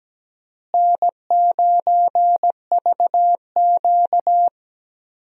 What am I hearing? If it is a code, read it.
Morse code, heard as a keyed tone, about 17 wpm, 700 Hz. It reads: N9VQ
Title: USA Call Signs - 4 Characters - Repeated 3x - 17wpm